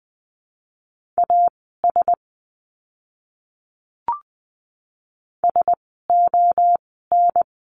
Morse code, heard as a keyed tone, about 20 wpm, spaced as for 14 wpm, 700 Hz. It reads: AS SON
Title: 061 Lesson 11f - Letter h - Words 20wpm